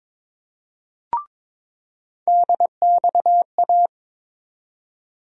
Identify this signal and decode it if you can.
Morse code, heard as a keyed tone, about 22 wpm, 700 Hz. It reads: DXA
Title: USA Call Sign Postfixes - 3 Letters 22wpm